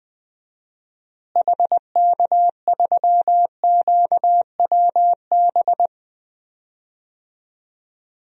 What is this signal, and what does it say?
Morse code, heard as a keyed tone, about 20 wpm, 700 Hz. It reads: HK3QWB
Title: DXCC-100 Call Signs - 6 Characters - Repeated 2x - 20wpm